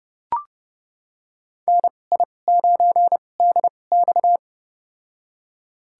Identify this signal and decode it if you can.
Morse code, heard as a keyed tone, about 31 wpm, spaced as for 22 wpm, 700 Hz. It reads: NI9DX
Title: USA Call Signs - 5 Characters - 2x Character Spacing - 30wpm